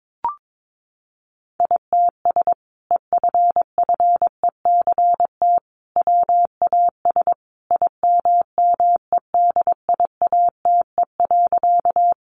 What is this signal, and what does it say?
Morse code, heard as a keyed tone, about 22 wpm, 700 Hz. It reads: ITS EFFECT WAS IMMEDIATE.
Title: Sentences from Top 1200 Words 22wpm